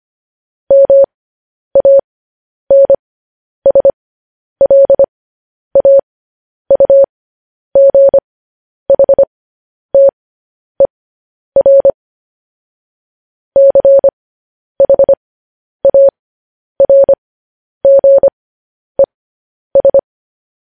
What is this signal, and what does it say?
Morse code, heard as a keyed tone, about 25 wpm, spaced as for 10 wpm, 550 Hz. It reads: MANSLAUGHTER CHARGES
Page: News Headlines in Morse Code at 10 WPM